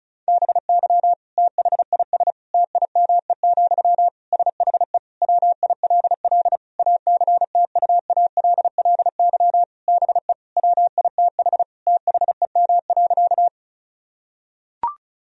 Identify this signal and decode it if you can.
Morse code, heard as a keyed tone, about 34 wpm, 700 Hz. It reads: BY THIS TIME, SHE WILL ACTUALLY BE WITH THEM.
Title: Sentences from Top 1000 Words - Review Part 4 - 35wpm